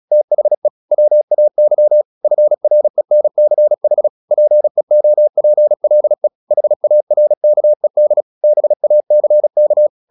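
Morse code, heard as a keyed tone, about 36 wpm, 600 Hz. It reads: THE WAY FRENCH PEOPLE HARKED BACK
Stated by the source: Morse practice at 36 wpm, text of Ken Follett's novel Never, from CW med Gnister.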